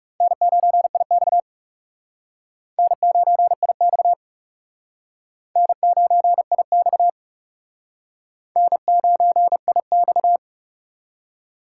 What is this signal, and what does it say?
Morse code, heard as a keyed tone, about 37 wpm, 700 Hz. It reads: N9IX N9IX N9IX N9IX
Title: Speed Racing - USA 1x2 Call Signs 30wpm